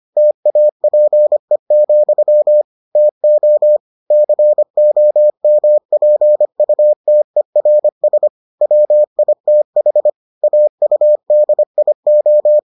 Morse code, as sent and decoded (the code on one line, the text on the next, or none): - .- .--. --..-- / - --- / -.-. --- -- .--. ..- - . .-. ... / .-- .. - .... / .- ..- -.. .. ---
TAPE, TO COMPUTERS WITH AUDIO